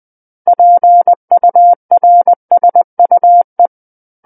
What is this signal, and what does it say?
Morse code, heard as a keyed tone, about 20 wpm, 700 Hz. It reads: PURSUE
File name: 20 WPM